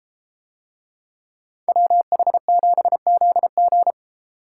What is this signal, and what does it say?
Morse code, heard as a keyed tone, about 30 wpm, 700 Hz. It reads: WH7ZG